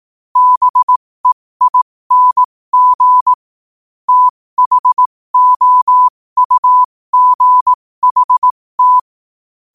Morse code, heard as a keyed tone, about 18 wpm, spaced as for 15 wpm, 1000 Hz. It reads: BEING THOUGHT
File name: qotdfiles / 15wpm